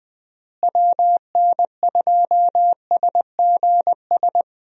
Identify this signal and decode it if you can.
Morse code, heard as a keyed tone, about 20 wpm, 700 Hz. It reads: WN2SGS
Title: USA Call Signs - 6 Characters - Repeated 3x - 20wpm